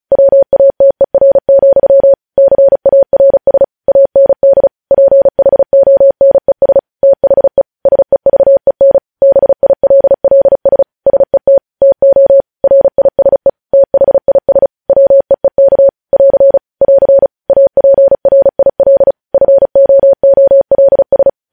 Morse code, heard as a keyed tone, about 36 wpm, 550 Hz. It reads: WATER, CARS AND PHONES THE SEVEN BILLS SET TO RISE THIS WEEK <AR> <AR> APRIL FOOLS
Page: news headlines in morse code at 35 wpm